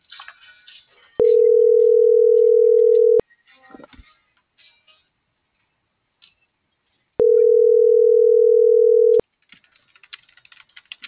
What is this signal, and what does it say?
Ringback tone (North America), 440 + 480 Hz.